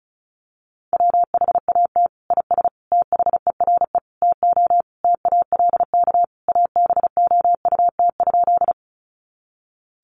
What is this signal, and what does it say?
Morse code, heard as a keyed tone, about 34 wpm, 700 Hz. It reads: WHAT IS THERE TO TALK ABOUT?
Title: Sentences from Top 400 Words 35wpm